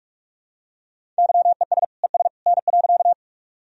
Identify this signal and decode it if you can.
Morse code, heard as a keyed tone, about 46 wpm, 700 Hz. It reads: YES ES N.
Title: Sentences from Top 500 Words and Abbreviations 45wpm